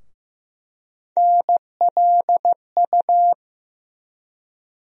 Morse code, heard as a keyed tone, about 15 wpm, 700 Hz. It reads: NLU
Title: USA Call Sign Postfixes - 3 Letters 15wpm